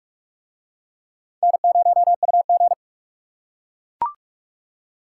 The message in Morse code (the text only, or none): N0UG